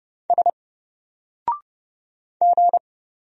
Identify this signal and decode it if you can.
Morse code, heard as a keyed tone, about 31 wpm, 700 Hz. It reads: S G
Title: Single Letter-Number Alt 30wpm